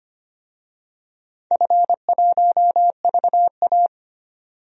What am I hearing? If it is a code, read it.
Morse code, heard as a keyed tone, about 25 wpm, 700 Hz. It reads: F1VA
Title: DXCC-100 Call Signs - 4 Characters - 25wpm